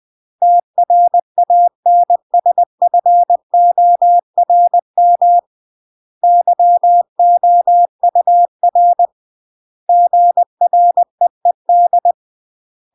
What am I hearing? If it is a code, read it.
Morse code, heard as a keyed tone, about 20 wpm, 700 Hz. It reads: TRANSFORM YOUR GREED
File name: CW / jodorowsky / 82 commandments 20-20